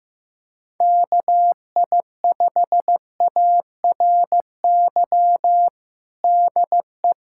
Morse code, heard as a keyed tone, about 15 wpm, 700 Hz. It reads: KI5ARY DE
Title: USA Callsign DE Callsign 15wpm